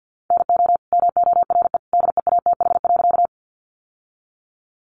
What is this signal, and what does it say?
Morse code, heard as a keyed tone, about 50 wpm, 700 Hz. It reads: NO MORE DEATH.
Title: Sentences from Top 1100 Words 50wpm